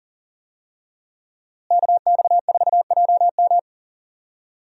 Morse code, heard as a keyed tone, about 40 wpm, 690 Hz. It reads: KX4JM
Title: USA Call Signs - 5 Characters - Repeated 3x - 40wpm